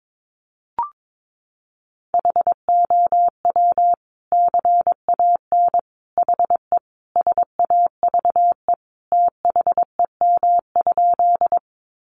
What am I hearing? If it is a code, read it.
Morse code, heard as a keyed tone, about 22 wpm, 700 Hz. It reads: HOW CAN HE SAVE THEM?
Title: Sentences from Top 1100 Words 22wpm